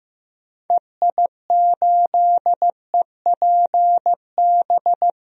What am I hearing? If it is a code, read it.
Morse code, heard as a keyed tone, about 15 wpm, 700 Hz. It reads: EI8EPB
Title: DXCC-100 Call Signs - 6 Characters - Repeated 2x - 15wpm